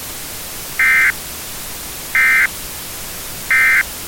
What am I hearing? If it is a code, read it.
Emergency Alert System (SAME) tones, heard as harsh buzzing bursts. It is heard from 0.7 to 3.9 s.